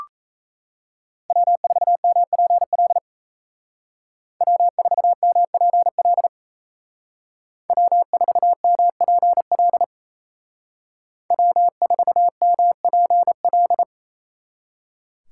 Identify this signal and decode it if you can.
Morse code, heard as a keyed tone, about 34 wpm, 700 Hz. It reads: W4MPL W4MPL W4MPL W4MPL
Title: Speed Racing - USA 1x3 Call Signs 28wpm